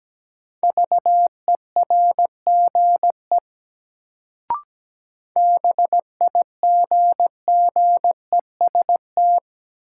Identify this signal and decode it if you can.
Morse code, heard as a keyed tone, about 17 wpm, 700 Hz. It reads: VERGE BIGGEST
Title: Sets of 4 Words - New Word Drill - 17wpm